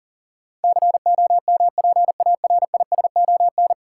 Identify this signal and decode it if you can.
Morse code, heard as a keyed tone, about 40 wpm, 700 Hz. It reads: COMPARISON
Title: Sentences from Top 1600 Words - New Word Drill Alt - 40wpm